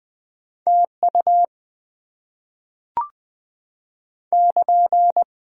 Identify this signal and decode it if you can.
Morse code, heard as a keyed tone, about 20 wpm, 700 Hz. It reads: TU <KN>
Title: Top 90 QSO Elements - 8x Character Spacing - 20wpm